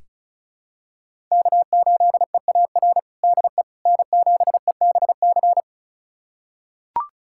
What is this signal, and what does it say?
Morse code, heard as a keyed tone, about 36 wpm, 700 Hz. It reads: K8EAR DE N7EBC
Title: USA Callsign DE Callsign - 5 Characters - 35wpm